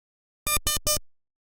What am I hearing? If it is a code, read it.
There is no code.